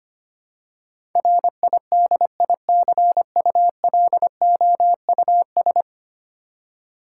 Morse code, heard as a keyed tone, about 25 wpm, 700 Hz. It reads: RIDICULOUS
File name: 10 Letter Words - First 3 Letters Given - 25wpm